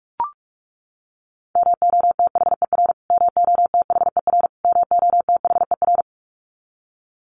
Morse code, heard as a keyed tone, about 44 wpm, 700 Hz. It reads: MOTHER MOTHER MOTHER
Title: Top 500 Words Repeated 45wpm